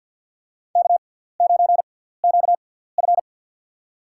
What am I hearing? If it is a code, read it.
Morse code, heard as a keyed tone, about 52 wpm, spaced as for 17 wpm, 690 Hz. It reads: K9QF
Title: USA Call Signs - 4 Characters - 6x Character Spacing - 50wpm